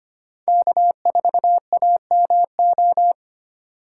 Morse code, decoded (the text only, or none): K4AMO